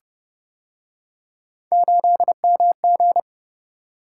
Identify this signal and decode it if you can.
Morse code, heard as a keyed tone, about 30 wpm, 700 Hz. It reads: 8MG